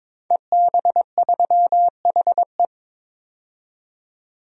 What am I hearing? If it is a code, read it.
Morse code, heard as a keyed tone, about 22 wpm, 700 Hz. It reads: EB3HE
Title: DXCC-100 Call Signs - 5 Characters - 22wpm